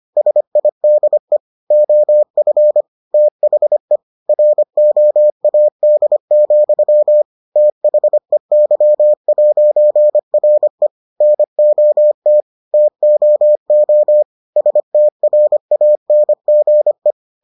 Morse code, heard as a keyed tone, about 25 wpm, 600 Hz. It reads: SIDE OF THE ROAD, THEY'RE NOT TOO STRANGE